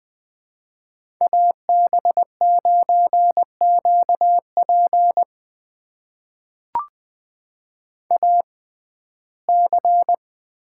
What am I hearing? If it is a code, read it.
Morse code, heard as a keyed tone, about 20 wpm, spaced as for 14 wpm, 700 Hz. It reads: AB9QP AC
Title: USA Call Signs - 5 Characters - 6x Character Spacing - 20wpm